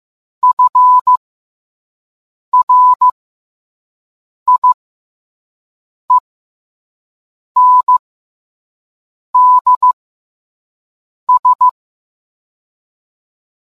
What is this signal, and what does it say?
Morse code, heard as a keyed tone, about 15 wpm, spaced as for 5 wpm, 1000 Hz. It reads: FRIENDS